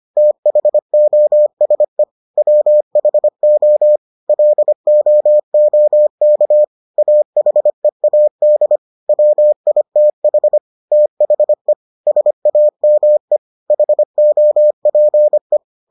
Morse code, heard as a keyed tone, about 25 wpm, 600 Hz. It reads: THOSE WHO LOOK AHEAD WITH THE SAME HOPE